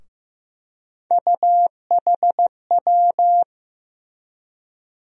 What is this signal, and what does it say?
Morse code, heard as a keyed tone, about 15 wpm, 700 Hz. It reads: UHW
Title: USA Call Sign Postfixes - 3 Letters 15wpm